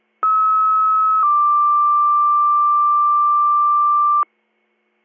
Two-tone page, heard as a one-second tone, then a three-second tone, about 1290 then 1180 Hz. It is heard from 0.2 to 4.2 s.
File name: 154265000.0